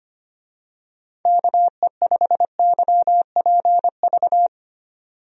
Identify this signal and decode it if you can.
Morse code, heard as a keyed tone, about 25 wpm, 700 Hz. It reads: KE5YPV
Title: USA Call Signs - 6 Characters - 25wpm